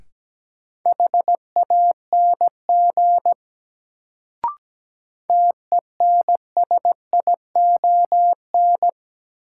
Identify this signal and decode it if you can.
Morse code, heard as a keyed tone, about 17 wpm, 700 Hz. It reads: HANG TENSION
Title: Sentences from Top 1600 Words - New Word Drill - 17wpm